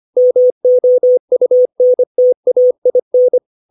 Morse code, heard as a keyed tone, about 25 wpm, 500 Hz. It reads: MOUNTAIN
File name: 632 words / 632 Words 25WPM